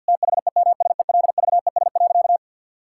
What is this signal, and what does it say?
Morse code, heard as a keyed tone, about 50 wpm, 700 Hz. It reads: THEMSELVES.
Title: Sentences from Top 1000 Words - Review Part 5 - 50wpm